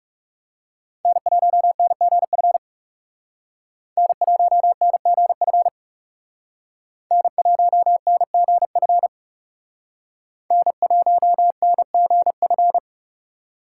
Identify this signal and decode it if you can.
Morse code, heard as a keyed tone, about 39 wpm, 700 Hz. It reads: N1NGF N1NGF N1NGF N1NGF